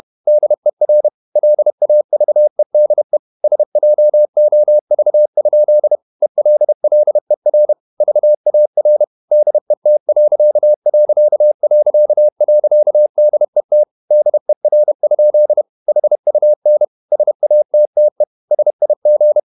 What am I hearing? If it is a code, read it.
Morse code, heard as a keyed tone, about 31 wpm, 600 Hz. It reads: DER LAVEDE SJOV? ELLER VAR DET....DET DER? HUN SATTE SIG
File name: Ferie QRQ 31wpm